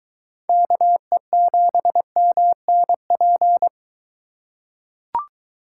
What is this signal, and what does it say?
Morse code, heard as a keyed tone, about 22 wpm, 700 Hz. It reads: KE7MNP